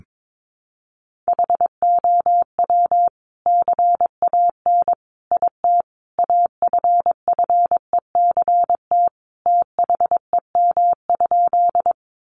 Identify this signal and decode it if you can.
Morse code, heard as a keyed tone, about 22 wpm, 700 Hz. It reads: HOW CAN IT AFFECT THEM?
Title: Sentences from Top 900 Words 22wpm